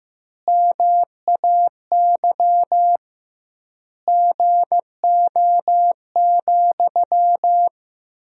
Morse code, heard as a keyed tone, about 15 wpm, 700 Hz. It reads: MAY GO,
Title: 2x Word-Spacing - Sentences from Top 400 Words - 15wpm